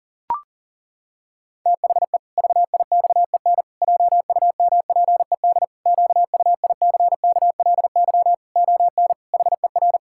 Morse code, heard as a keyed tone, about 40 wpm, 700 Hz. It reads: THE VIXEN JUMPED QUICKLY ON HER
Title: Pangram Challenge v2 - 40wpm